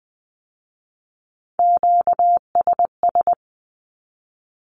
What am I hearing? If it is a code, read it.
Morse code, heard as a keyed tone, about 20 wpm, 700 Hz. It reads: QSS
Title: Q-Codes with Spoken Meaning - 20wpm